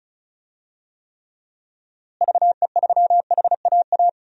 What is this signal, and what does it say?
Morse code, heard as a keyed tone, about 34 wpm, 700 Hz. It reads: VE3HAA
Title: DXCC-100 Call Signs - 6 Characters - Repeated 2x - 35wpm